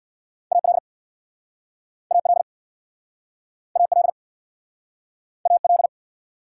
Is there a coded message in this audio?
Yes: Morse code from 0.5 to 5.9 s, heard as a keyed tone, about 63 wpm, 700 Hz. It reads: AL AL AL AL